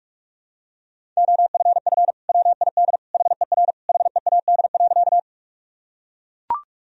Morse code, heard as a keyed tone, about 41 wpm, 700 Hz. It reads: OFF WID HER HEAD.
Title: Sentences from Top 300 Words and Abbreviations 45wpm